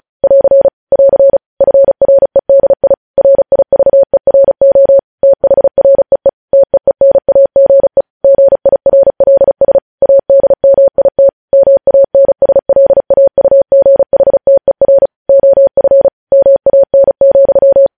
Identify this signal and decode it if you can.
Morse code, heard as a keyed tone, about 36 wpm, 550 Hz. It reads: <AR> <AR> FREDI RIVERO THREE TEENAGE GIRLS ADMIT MANSLAUGHTER OF MAN,
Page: news headlines in morse code at 35 wpm